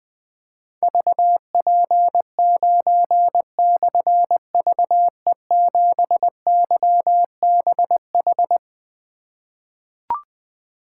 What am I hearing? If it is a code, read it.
Morse code, heard as a keyed tone, about 20 wpm, 700 Hz. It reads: VP9/VE7YBH
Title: Difficult Call Signs 20wpm